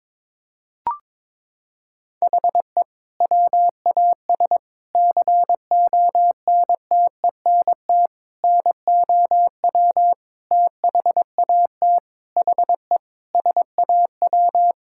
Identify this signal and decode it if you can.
Morse code, heard as a keyed tone, about 22 wpm, 700 Hz. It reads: HE WAS CONTENT NOW THAT HE SAW
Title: Sentences from Top 1000 Words - Review Part 6 - 22wpm